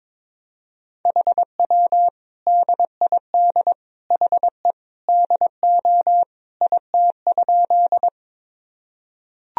Morse code, heard as a keyed tone, about 22 wpm, 700 Hz. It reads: HW DID HE DO IT?
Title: Sentences from Top 100 Words and Abbreviations 22wpm